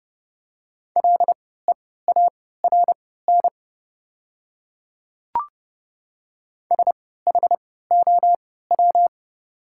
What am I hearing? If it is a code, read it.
Morse code, heard as a keyed tone, about 30 wpm, spaced as for 17 wpm, 700 Hz. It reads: LEARN SHOW
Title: 3x Character Spacing - Top 300 Words - 30wpm